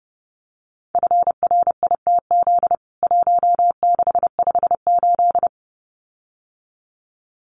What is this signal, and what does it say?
Morse code, heard as a keyed tone, about 30 wpm, 700 Hz. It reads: FRITZ 1658
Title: CWT Exchange 30wpm